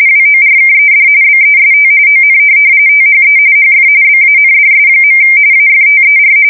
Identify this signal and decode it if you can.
RTTY, about 45.45 baud. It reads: WORKED BEST ON THE HF BANDS. AFTER
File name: fsk170 / teletype-RTTY